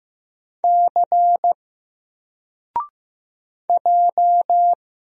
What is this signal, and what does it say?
Morse code, heard as a keyed tone, about 15 wpm, 700 Hz. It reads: C J
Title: Single Character (Letter:Number:Punctuation:Prosign) 15wpm